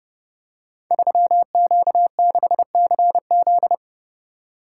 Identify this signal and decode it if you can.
Morse code, heard as a keyed tone, about 30 wpm, 700 Hz. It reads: 3Q6CZ
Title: Code Groups - 5 Letter:Numbers - 30wpm